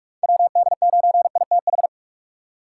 Morse code, heard as a keyed tone, about 44 wpm, 700 Hz. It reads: WD9ITH
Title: USA Call Signs - 6 Characters - 4x Character Spacing - 45wpm